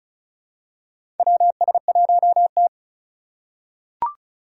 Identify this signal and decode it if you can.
Morse code, heard as a keyed tone, about 36 wpm, 700 Hz. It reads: WS1T
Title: USA Call Signs - 3x Character Spacing - 35wpm